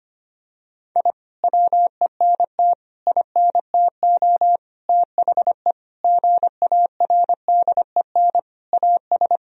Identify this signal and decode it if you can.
Morse code, heard as a keyed tone, about 25 wpm, 700 Hz. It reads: I WENT INTO THE GARDEN AS
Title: Sentences from Top 1000 Words - Review Part 4 - 25wpm